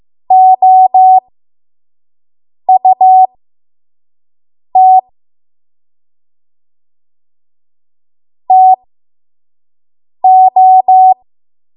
Morse code, heard as a keyed tone, about 15 wpm, spaced as for 5 wpm, 750 Hz. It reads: OUT TO
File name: Archive / 5 WPM / 05WPM